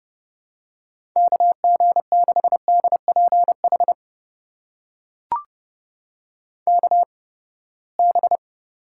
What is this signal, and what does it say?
Morse code, heard as a keyed tone, about 30 wpm, spaced as for 21 wpm, 700 Hz. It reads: KG6DPH KB
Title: USA Call Signs - 6 Characters - 8x Character Spacing - 30wpm